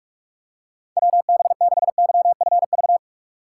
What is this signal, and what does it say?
Morse code, heard as a keyed tone, about 45 wpm, 700 Hz. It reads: WB6YFV